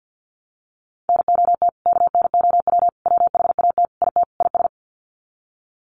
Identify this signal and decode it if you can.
Morse code, heard as a keyed tone, about 50 wpm, 700 Hz. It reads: NOT KNOW WHAT IT IS
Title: Sets of 5 Words Encore 50wpm